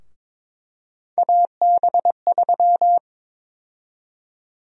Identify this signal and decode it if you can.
Morse code, heard as a keyed tone, about 22 wpm, 700 Hz. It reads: AB3
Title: USA Call Sign Prefixes - Repeated 2x - 22wpm